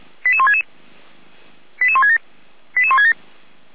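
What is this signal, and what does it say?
Five-tone selective call (ZVEI), heard as quick runs of five tones, tones of about 70 ms. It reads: 88166 88167 88167
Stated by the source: ZVEI five-tone call received by SDR, recorded by feuerwehr-orlamuende (fire brigade Orlamünde).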